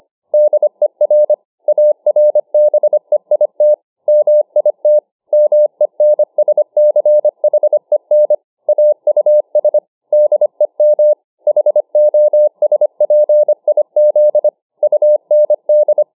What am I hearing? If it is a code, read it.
Morse code, heard as a keyed tone, about 25 wpm, 600 Hz. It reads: DER ARBEIT MIT MENSCHEN AUS DEM HOSPIZ UND